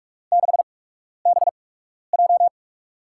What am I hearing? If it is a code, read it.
Morse code, heard as a keyed tone, about 41 wpm, spaced as for 12 wpm, 700 Hz. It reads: 6BJ
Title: USA Call Signs - 5 Characters - 8x Character Spacing - 45wpm